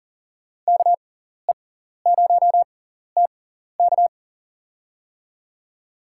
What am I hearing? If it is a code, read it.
Morse code, heard as a keyed tone, about 40 wpm, spaced as for 14 wpm, 700 Hz. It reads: KE0TK
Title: USA Call Signs - 5 Characters - 6x Character Spacing - 40wpm